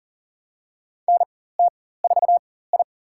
Morse code, heard as a keyed tone, about 40 wpm, spaced as for 19 wpm, 700 Hz. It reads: NT4I
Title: USA Call Signs - 4 Characters - 4x Character Spacing - 40wpm